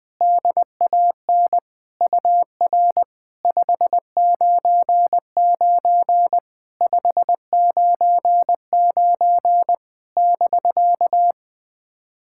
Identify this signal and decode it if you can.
Morse code, heard as a keyed tone, about 20 wpm, 660 Hz. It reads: DAN UR 599 599 <BK>
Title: SOTA QSO Practice 20wpm